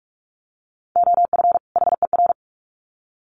Morse code, heard as a keyed tone, about 44 wpm, 700 Hz. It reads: OF HER
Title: Sets of 2 Words 45wpm